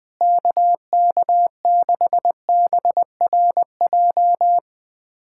Morse code, heard as a keyed tone, about 21 wpm, 700 Hz. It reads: KK6BRJ
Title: USA Callsign DE Callsign - 6 Characters - 20wpm